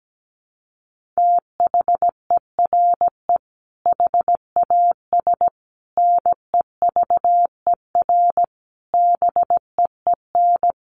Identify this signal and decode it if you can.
Morse code, heard as a keyed tone, about 17 wpm, 700 Hz. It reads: THERE HAS NEVER BEEN
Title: Sentences from Top 1500 Words 17wpm